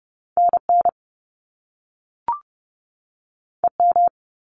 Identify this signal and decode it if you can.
Morse code, heard as a keyed tone, about 30 wpm, 700 Hz. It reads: NN EM
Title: Top Two Letter Combinations in Words 30wpm